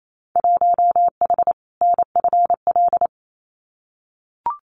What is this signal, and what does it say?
Morse code, heard as a keyed tone, about 28 wpm, 700 Hz. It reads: 1H NFL
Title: Winter Field Day Exchanges 28wpm